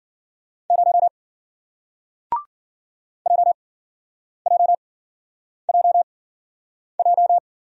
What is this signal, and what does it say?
Morse code, heard as a keyed tone, about 51 wpm, spaced as for 9 wpm, 700 Hz. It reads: 0 JJJJ